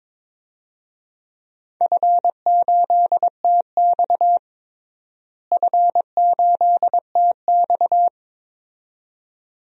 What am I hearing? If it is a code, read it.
Morse code, heard as a keyed tone, about 22 wpm, 700 Hz. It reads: F8TX F8TX
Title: DXCC-100 Call Signs - 4 Characters - Repeated 2x - 22wpm